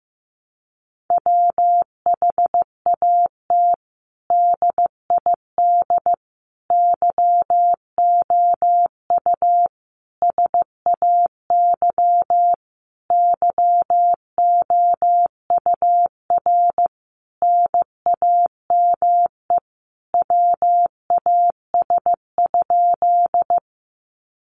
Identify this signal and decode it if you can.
Morse code, heard as a keyed tone, about 15 wpm, 700 Hz. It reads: WHAT DID YOU SAY YOUR NAME WAS?
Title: Sentences from Top 200 Words 15wpm